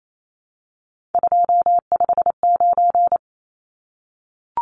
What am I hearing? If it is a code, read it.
Morse code, heard as a keyed tone, about 28 wpm, 700 Hz. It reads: 259
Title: Numbers - 3-Digit 28wpm